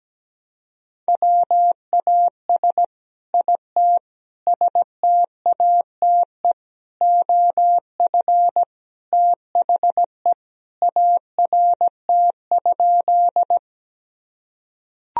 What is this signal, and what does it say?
Morse code, heard as a keyed tone, about 17 wpm, 700 Hz. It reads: WAS IT STATE OF THE ART?
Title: Sentences from Top 1000 Words - Review Part 3 - 17wpm